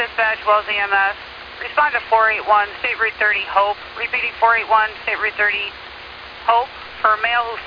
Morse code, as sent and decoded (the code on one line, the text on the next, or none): none